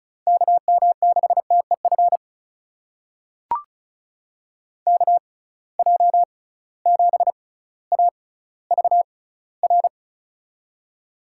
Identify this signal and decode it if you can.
Morse code, heard as a keyed tone, about 36 wpm, spaced as for 18 wpm, 700 Hz. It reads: KM6TEF KJ7AVR
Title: USA Call Signs - 6 Characters - 6x Character Spacing - 35wpm